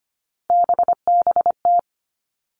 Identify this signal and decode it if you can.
Morse code, heard as a keyed tone, about 25 wpm, 700 Hz. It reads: BBT
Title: Three Challenging Letter-Number Combinations 25wpm